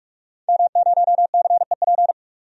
Morse code, heard as a keyed tone, about 47 wpm, 700 Hz. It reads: M0CEP
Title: DXCC-100 Call Signs - 5 Characters - 45wpm